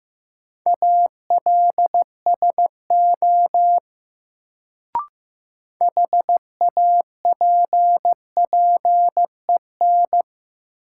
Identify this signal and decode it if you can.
Morse code, heard as a keyed tone, about 15 wpm, 700 Hz. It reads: ALSO HAPPEN
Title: Sentences from Top 400 Words - New Word Drill - 15wpm